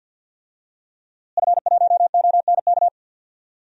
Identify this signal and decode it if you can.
Morse code, heard as a keyed tone, about 51 wpm, 700 Hz. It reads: F1ONK
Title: DXCC-100 Call Signs - 5 Characters - Repeated 2x - 50wpm